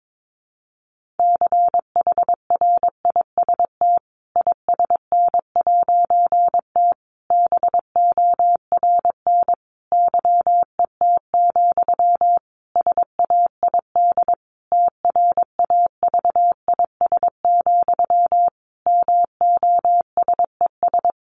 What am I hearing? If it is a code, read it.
Morse code, heard as a keyed tone, about 22 wpm, 700 Hz. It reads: CHRIST ISN'T BORN YET, SAID TRAVIS, MOSES
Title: A Sound Of Thunder 22wpm